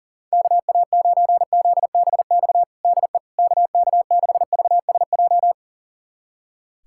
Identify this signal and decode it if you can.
Morse code, heard as a keyed tone, about 40 wpm, 700 Hz. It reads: KA9ZBX DE KK6VSJ
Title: USA Callsign DE Callsign - 6 Characters - 40wpm